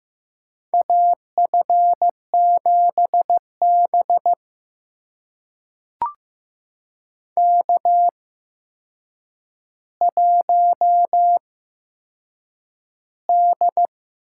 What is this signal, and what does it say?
Morse code, heard as a keyed tone, about 15 wpm, spaced as for 7 wpm, 700 Hz. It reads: AF7B K1D
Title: USA Call Signs - 4 Characters - 8x Character Spacing - 15wpm